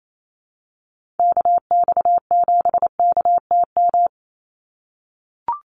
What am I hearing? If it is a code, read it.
Morse code, heard as a keyed tone, about 28 wpm, 700 Hz. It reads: KX7KTM
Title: USA Call Signs 28wpm